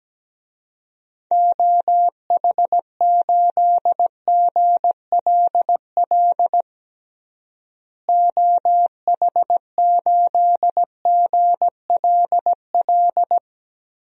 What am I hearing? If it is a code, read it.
Morse code, heard as a keyed tone, about 17 wpm, 700 Hz. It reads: OH8GLL OH8GLL
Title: DXCC-100 Call Signs - 6 Characters - Repeated 3x - 17wpm